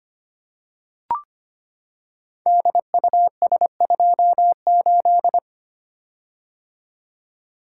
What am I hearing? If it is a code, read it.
Morse code, heard as a keyed tone, about 25 wpm, 690 Hz. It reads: DUS28
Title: Code Groups - 5 Letter:Numbers - 25wpm